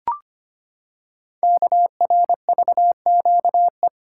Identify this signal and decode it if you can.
Morse code, heard as a keyed tone, about 25 wpm, 700 Hz. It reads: KRVQE
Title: Code Groups - 5 Letters - 25wpm